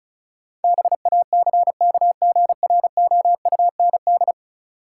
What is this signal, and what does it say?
Morse code, heard as a keyed tone, about 36 wpm, 700 Hz. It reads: BACKGROUND
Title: Sentences from Top 1300 Words - New Word Drill Alt - 35wpm